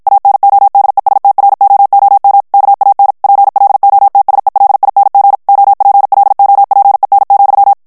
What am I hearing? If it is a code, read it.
Morse code, heard as a keyed tone, about 50 wpm, 800 Hz. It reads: ANODESTROOM KAN PLOTSELING OPLOPEN,
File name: HOLLANDAIS texte clair 2 vitesse 50 mots minute